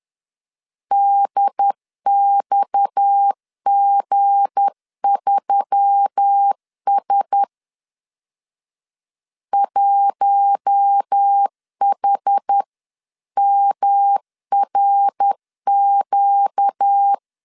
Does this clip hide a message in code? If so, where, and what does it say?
Yes: Morse code from 0.9 to 17.1 s, heard as a keyed tone, about 10 wpm, 800 Hz. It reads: DXG3S 1HMRQ